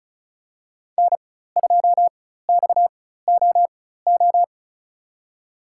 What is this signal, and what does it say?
Morse code, heard as a keyed tone, about 36 wpm, spaced as for 16 wpm, 700 Hz. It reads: N2XOO
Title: USA Call Signs - 5 Characters - 4x Character Spacing - 35wpm